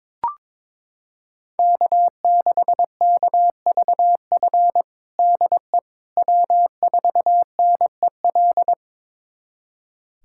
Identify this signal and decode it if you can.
Morse code, heard as a keyed tone, about 22 wpm, 700 Hz. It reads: K6KVF DE W4NEL
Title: USA Callsign DE Callsign - 5 Characters - 22wpm